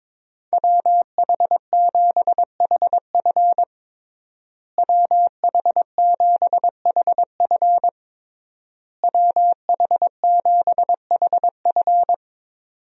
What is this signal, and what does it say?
Morse code, heard as a keyed tone, about 22 wpm, 700 Hz. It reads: WH7HF WH7HF WH7HF
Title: USA Call Signs - 5 Characters - Repeated 3x - 22wpm